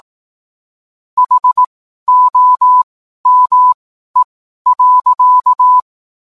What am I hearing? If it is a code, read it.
Morse code, heard as a keyed tone, about 18 wpm, spaced as for 12 wpm, 1000 Hz. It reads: HOME.